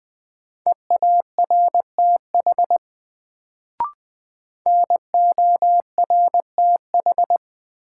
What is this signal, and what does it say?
Morse code, heard as a keyed tone, about 20 wpm, 700 Hz. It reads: EARTH NORTH
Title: Sets of 5 Words Encore - New Word Drill - 20wpm